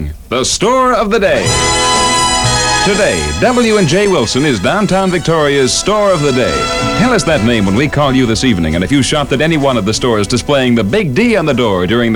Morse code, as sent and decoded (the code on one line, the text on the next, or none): none